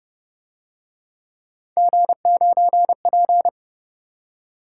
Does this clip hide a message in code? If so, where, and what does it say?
Yes: Morse code from 1.8 to 3.5 s, heard as a keyed tone, about 32 wpm, 700 Hz. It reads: G9P